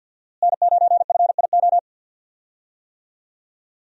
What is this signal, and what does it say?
Morse code, heard as a keyed tone, about 52 wpm, 700 Hz. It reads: N9FIO
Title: USA Call Signs - 5 Characters - Repeated 3x - 50wpm